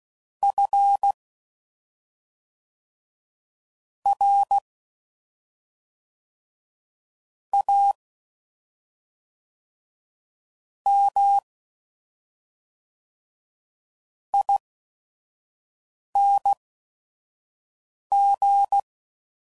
Morse code, heard as a keyed tone, about 16 wpm, spaced as for 5 wpm, 800 Hz. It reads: F R A M ING
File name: Suffixes / Special ING's / SlowToFaster ING's